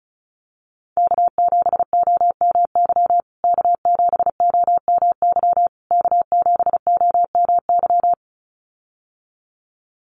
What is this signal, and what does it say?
Morse code, heard as a keyed tone, about 34 wpm, 700 Hz. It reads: K7OMY K7OMY K7OMY
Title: USA Call Signs Repeated 3x 35wpm